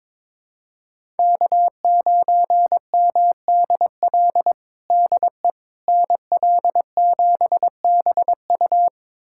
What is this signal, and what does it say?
Morse code, heard as a keyed tone, about 22 wpm, 700 Hz. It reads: K9MDL DE NL7BU
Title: USA Callsign DE Callsign - 5 Characters - 22wpm